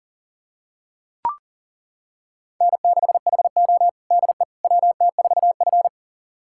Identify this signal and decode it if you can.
Morse code, heard as a keyed tone, about 40 wpm, 700 Hz. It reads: N6HO DE WT4F
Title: USA Callsign DE Callsign - 4 Characters - 40wpm